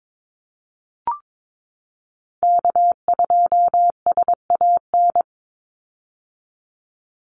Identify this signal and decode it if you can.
Morse code, heard as a keyed tone, about 22 wpm, 700 Hz. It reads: K2SAN
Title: Call Signs - 22wpm